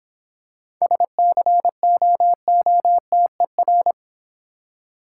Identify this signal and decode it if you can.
Morse code, heard as a keyed tone, about 26 wpm, 700 Hz. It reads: SCOOTER